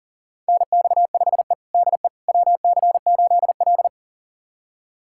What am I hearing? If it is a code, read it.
Morse code, heard as a keyed tone, about 40 wpm, 700 Hz. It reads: NX5E DE WC8L
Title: USA Callsign DE Callsign - 4 Characters - 40wpm